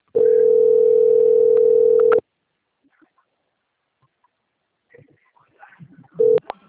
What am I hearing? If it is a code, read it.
Ringback tone (North America), 440 + 480 Hz.